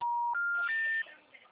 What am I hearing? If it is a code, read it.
Special information tone.
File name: out- -2012- - - .99